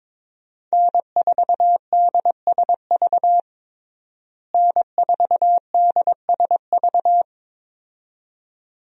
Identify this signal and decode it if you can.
Morse code, heard as a keyed tone, about 22 wpm, 700 Hz. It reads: N4DSV N4DSV
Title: USA Call Signs - 5 Characters - Repeated 2x - 22wpm